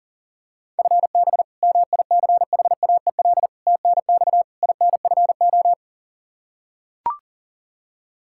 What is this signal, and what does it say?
Morse code, heard as a keyed tone, about 37 wpm, 700 Hz. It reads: FB MICHAEL TNX INFO
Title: Practice QSO - Exchange 3 - More Info - 40wpm